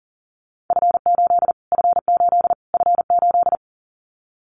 Morse code, heard as a keyed tone, about 37 wpm, 700 Hz. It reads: F8 F8 F8
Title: Callsign Prefixes for Top 25 DXCC - Repeated 3x - 40wpm